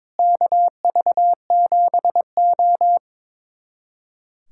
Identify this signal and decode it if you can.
Morse code, heard as a keyed tone, about 22 wpm, 700 Hz. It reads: KV7O
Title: USA Callsign DE Callsign - 4 Characters - 22wpm